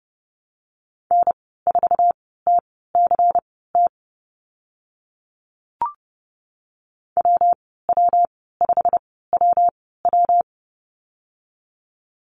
Morse code, heard as a keyed tone, about 30 wpm, spaced as for 17 wpm, 700 Hz. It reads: N4TCT WW5WW
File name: Lesson 19g - Letter w - Call Signs 30wpm